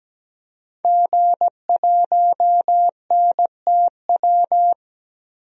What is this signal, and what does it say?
Morse code, heard as a keyed tone, about 17 wpm, 700 Hz. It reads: G1NTW
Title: DXCC-100 Call Signs - 5 Characters - 17wpm